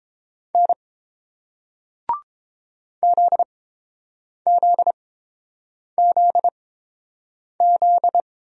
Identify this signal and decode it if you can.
Morse code, heard as a keyed tone, about 31 wpm, spaced as for 7 wpm, 690 Hz. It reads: N ZZZZ